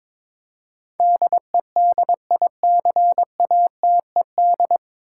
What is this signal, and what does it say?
Morse code, heard as a keyed tone, about 22 wpm, 700 Hz. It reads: DEDICATED
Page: Morse Code Ninja practice 22wpm